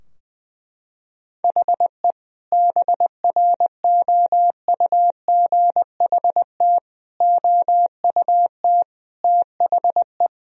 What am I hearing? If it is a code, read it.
Morse code, heard as a keyed tone, about 20 wpm, 700 Hz. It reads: HE BROUGHT OUT THE
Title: Sentences from Top 1000 Words - Review Part 3 - 20wpm